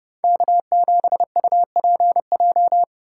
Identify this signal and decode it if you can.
Morse code, heard as a keyed tone, about 30 wpm, 700 Hz. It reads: K7UPJ
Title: K1USN SST QSO Practice - 30wpm